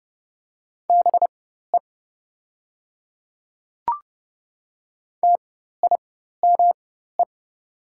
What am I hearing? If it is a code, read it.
Morse code, heard as a keyed tone, about 32 wpm, spaced as for 14 wpm, 700 Hz. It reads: BE TIME
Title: 4x Character Spacing - Top 300 Words - 30wpm